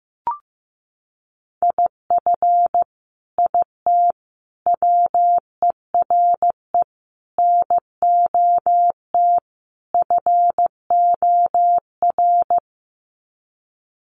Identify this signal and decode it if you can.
Morse code, heard as a keyed tone, about 15 wpm, 700 Hz. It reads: IF IT WERE NOT FOR
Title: Sets of 5 Words 15wpm